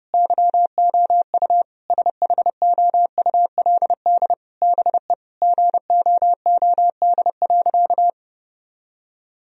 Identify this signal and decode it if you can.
Morse code, heard as a keyed tone, about 32 wpm, 700 Hz. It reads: YOU SHOULD BE GOOD.
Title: Sentences from Top 1000 Words - Review Part 1 - 30wpm